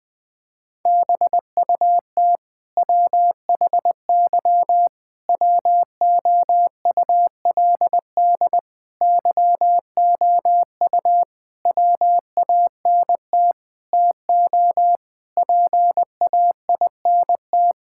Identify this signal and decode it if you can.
Morse code, heard as a keyed tone, about 20 wpm, 700 Hz. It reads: BUT WHY WOULD YOU WANT TO PAINT
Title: Sentences from Top 1000 Words - Review Part 6 - 20wpm